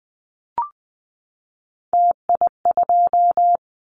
Morse code, heard as a keyed tone, about 20 wpm, 700 Hz. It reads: TI2
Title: Callsign Prefixes for Top 100 DXCC - 20wpm